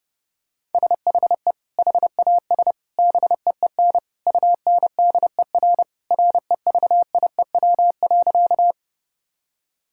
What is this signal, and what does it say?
Morse code, heard as a keyed tone, about 30 wpm, 700 Hz. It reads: SHE HAS BEEN UNDER REVIEW.
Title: Sentences from Top 1000 Words - Review Part 6 - 30wpm